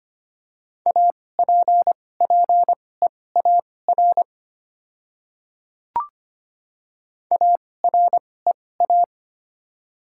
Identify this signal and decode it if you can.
Morse code, heard as a keyed tone, about 25 wpm, spaced as for 18 wpm, 700 Hz. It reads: APPEAR AREA
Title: 2x Character Spacing - Top 500 Words - 25wpm